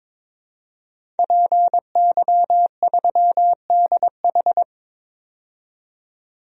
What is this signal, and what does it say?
Morse code, heard as a keyed tone, about 22 wpm, 700 Hz. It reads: PY3DH
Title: DXCC-100 Call Signs - 5 Characters - 22wpm